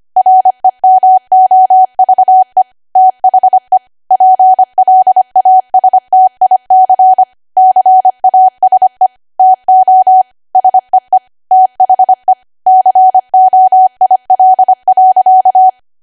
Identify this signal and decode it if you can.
Morse code, heard as a keyed tone, about 25 wpm, 750 Hz. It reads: REMOVE THE PLASTIC CASE TO SEE THE COIL.